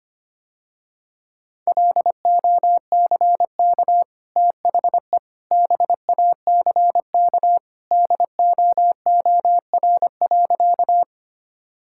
Morse code, heard as a keyed tone, about 25 wpm, 700 Hz. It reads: LOCK THE BACK DOOR.
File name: Easy Sentences with Hints - 25wpm